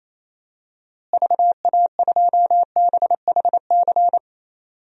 Morse code, heard as a keyed tone, about 27 wpm, 700 Hz. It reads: VA2BHC
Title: DXCC-100 Call Signs - 6 Characters - Repeated 2x - 28wpm